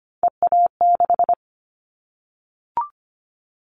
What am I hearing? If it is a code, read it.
Morse code, heard as a keyed tone, about 25 wpm, 700 Hz. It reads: EA6
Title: Callsign Prefixes for Top 50 DXCC - 25wpm